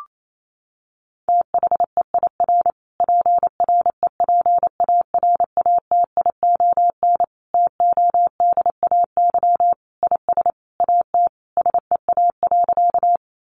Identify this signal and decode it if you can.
Morse code, heard as a keyed tone, about 28 wpm, 700 Hz. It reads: THEIR PREPARATION TODAY IS AT SEA.
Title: Sentences from Top 1400 Words 28wpm